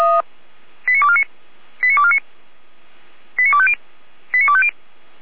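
Five-tone selective call (ZVEI), heard as quick runs of five tones, tones of about 70 ms. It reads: 89249 89249 89250 89250
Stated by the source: ZVEI five-tone selective call received on SDR, fire brigade alerting, Feuerwehr Orlamünde.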